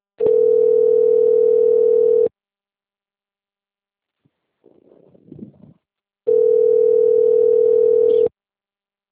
Ringback tone (North America), 440 + 480 Hz.